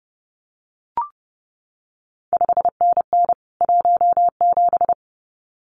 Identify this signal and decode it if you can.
Morse code, heard as a keyed tone, about 30 wpm, 700 Hz. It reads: 5NN 17
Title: CQ WW DX Contest Exchange Encore - 30wpm